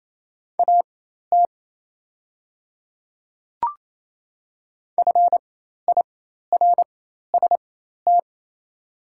Morse code, heard as a keyed tone, about 27 wpm, spaced as for 13 wpm, 700 Hz. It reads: AT FIRST
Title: 4x Character Spacing - Top 300 Words - 28wpm